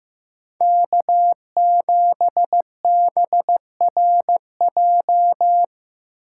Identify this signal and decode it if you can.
Morse code, heard as a keyed tone, about 15 wpm, 700 Hz. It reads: K7BRJ
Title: USA Call Signs - 5 Characters - 8x Character Spacing - 15wpm